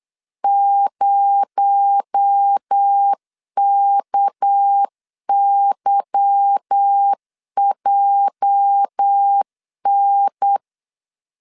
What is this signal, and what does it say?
Morse code, heard as a keyed tone, about 8 wpm, 800 Hz. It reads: 0KYJN
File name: LEZIONI / 8 YJ45 / 8 YJ 8wpm 3313